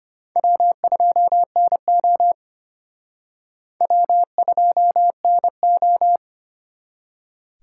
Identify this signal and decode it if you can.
Morse code, heard as a keyed tone, about 31 wpm, 700 Hz. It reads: W2NO W2NO
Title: Speed Racing - USA Call Signs 25wpm